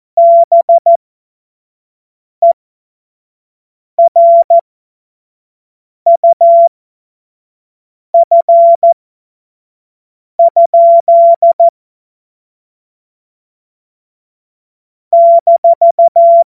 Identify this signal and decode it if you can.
Morse code, heard as a keyed tone, about 14 wpm, spaced as for 5 wpm, 680 Hz. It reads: BERUF? -
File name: Morsekurs / Witze / Witz 26 März 2024 25 F70